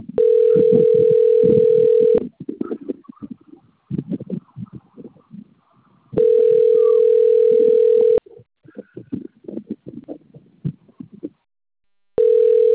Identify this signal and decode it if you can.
Ringback tone (North America), 440 + 480 Hz.
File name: rg-1600-unknown